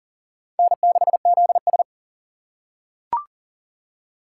Morse code, heard as a keyed tone, about 40 wpm, 700 Hz. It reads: N6ZS
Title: USA Call Signs - 4x Character Spacing - 40wpm